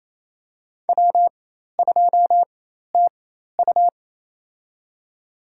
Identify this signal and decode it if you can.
Morse code, heard as a keyed tone, about 27 wpm, spaced as for 13 wpm, 700 Hz. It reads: W2TU
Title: USA Call Signs - 4x Character Spacing - 28wpm